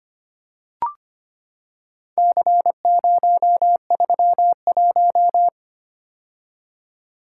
Morse code, heard as a keyed tone, about 25 wpm, 700 Hz. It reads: C031
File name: practice / 25wpm / Four Letter-Number Combinations - 25wpm